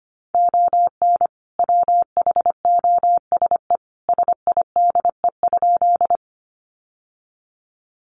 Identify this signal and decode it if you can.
Morse code, heard as a keyed tone, about 25 wpm, 700 Hz. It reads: ON WHOSE SIDE?
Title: Sentences from Top 1200 Words 25wpm